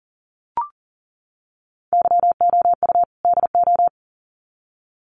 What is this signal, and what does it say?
Morse code, heard as a keyed tone, about 40 wpm, 700 Hz. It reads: YOU DO
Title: Sets of 2 Words 40wpm